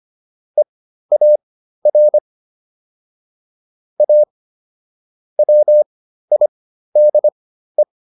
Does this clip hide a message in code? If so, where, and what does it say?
Yes: Morse code from 0.6 to 7.8 s, heard as a keyed tone, about 25 wpm, spaced as for 13 wpm, 600 Hz. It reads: EAR A WIDE